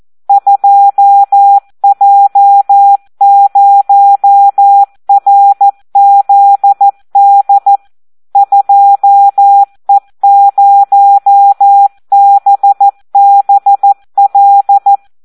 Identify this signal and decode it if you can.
Morse code, heard as a keyed tone, about 14 wpm, 800 Hz. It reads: 2J0RZD 2E0BBL
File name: Audio practica CW / SD 1 14